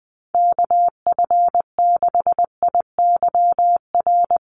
Morse code, heard as a keyed tone, about 19 wpm, 700 Hz. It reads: KF6IYR
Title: USA Call Signs Repeated 3x 20wpm